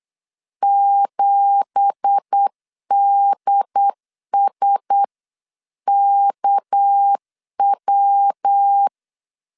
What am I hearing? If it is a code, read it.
Morse code, heard as a keyed tone, about 8 wpm, 800 Hz. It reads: 7DSKW
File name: CW / DET / 1 9 1500C S B 8wpm 5624